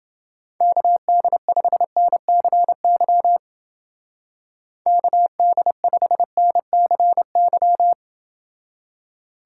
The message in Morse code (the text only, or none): KD5NCY KD5NCY